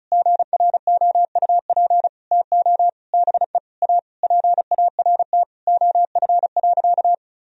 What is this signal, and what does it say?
Morse code, heard as a keyed tone, about 36 wpm, 700 Hz. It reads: GROUP TO BE A PART OF.
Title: Sentences from Top 1000 Words - Review Part 1 - 35wpm